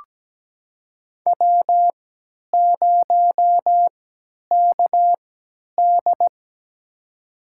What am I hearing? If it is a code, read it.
Morse code, heard as a keyed tone, about 17 wpm, spaced as for 10 wpm, 700 Hz. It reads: W0KD
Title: USA Call Signs - 4 Characters - 3x Character Spacing - 17wpm